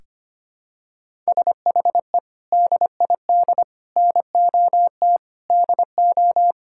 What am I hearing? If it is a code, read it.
Morse code, heard as a keyed tone, about 25 wpm, 700 Hz. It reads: SHE DID NOT DO
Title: Sentences from Top 1000 Words - Review Part 5 - 25wpm